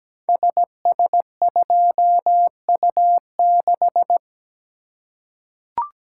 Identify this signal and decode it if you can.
Morse code, heard as a keyed tone, about 17 wpm, 700 Hz. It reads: SS2U6
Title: Code Groups - 5 Letter:Numbers - 17wpm